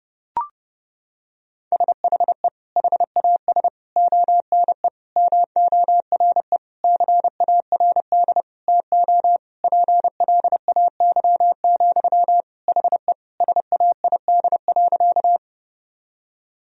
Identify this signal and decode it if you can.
Morse code, heard as a keyed tone, about 30 wpm, 700 Hz. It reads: SHE HAS ONE MORE CARD TO PLAY, HE SAID.
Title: Sentences from Top 1000 Words - Review Part 4 - 30wpm